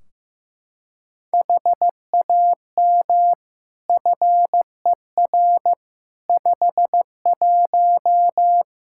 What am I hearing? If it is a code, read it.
Morse code, heard as a keyed tone, about 15 wpm, 700 Hz. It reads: HAM FER 51
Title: Practice QSO - Exchange 3 - More Info - 15wpm